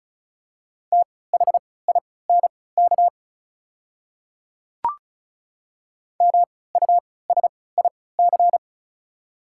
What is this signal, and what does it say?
Morse code, heard as a keyed tone, about 36 wpm, spaced as for 20 wpm, 700 Hz. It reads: THINK MUSIC
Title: 3x Character Spacing - Top 300 Words - 35wpm